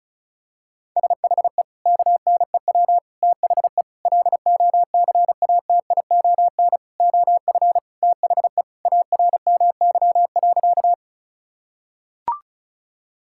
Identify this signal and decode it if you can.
Morse code, heard as a keyed tone, about 36 wpm, 690 Hz. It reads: SHE KNEW THE LOCATION OF THE ARMY.
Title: Sentences from Top 1000 Words - Review Part 6 - 35wpm